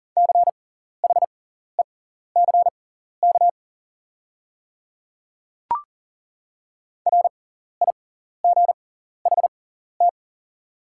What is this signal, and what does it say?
Morse code, heard as a keyed tone, about 40 wpm, spaced as for 14 wpm, 690 Hz. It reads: CHECK RIGHT
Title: 6x Character Spacing - Top 500 Words - 40wpm